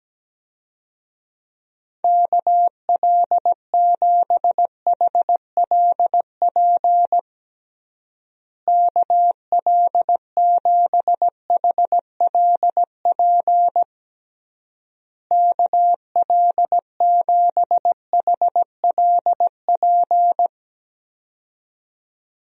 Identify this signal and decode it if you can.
Morse code, heard as a keyed tone, about 17 wpm, 700 Hz. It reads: KL7HLP KL7HLP KL7HLP
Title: DXCC-100 Call Signs - 6 Characters - Repeated 3x - 17wpm